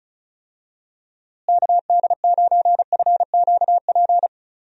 Morse code, heard as a keyed tone, about 36 wpm, 700 Hz. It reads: KD9FQP
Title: USA Call Signs - 6 Characters - Repeated 2x - 35wpm